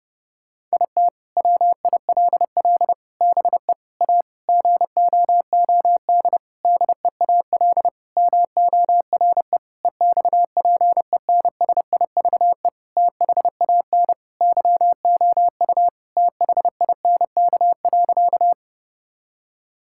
Morse code, heard as a keyed tone, about 30 wpm, 700 Hz. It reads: IT WILL BE A GOOD DEAL MORE EXPENSIVE THAN YOU THINK.
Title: Sentences from Top 1000 Words - Review Part 6 - 30wpm